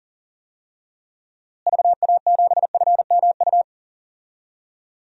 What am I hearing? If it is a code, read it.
Morse code, heard as a keyed tone, about 40 wpm, 700 Hz. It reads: VA7FMU
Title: DXCC-100 Call Signs - 6 Characters - Repeated 2x - 40wpm